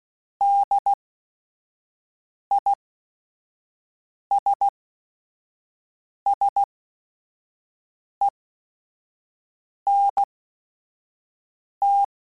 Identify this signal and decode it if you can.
Morse code, heard as a keyed tone, about 16 wpm, spaced as for 5 wpm, 800 Hz. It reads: DISSENT